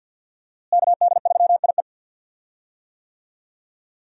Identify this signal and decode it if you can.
Morse code, heard as a keyed tone, about 47 wpm, 700 Hz. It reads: KD3IE
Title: USA Call Signs - 5 Characters - Repeated 3x - 50wpm